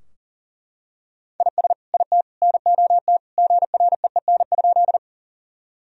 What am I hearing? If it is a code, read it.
Morse code, heard as a keyed tone, about 37 wpm, 700 Hz. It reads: IS IT NOT GREEN?
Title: Sentences from Top 1000 Words - Review Part 2 - 40wpm